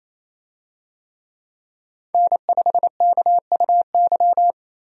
Morse code, heard as a keyed tone, about 28 wpm, 700 Hz. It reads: N5KUY
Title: USA Call Signs - 5 Characters - Repeated 3x - 28wpm